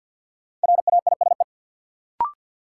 Morse code, heard as a keyed tone, about 50 wpm, 700 Hz. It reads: RAISE
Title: Sentences from Top 1400 Words - New Word Drill - 50wpm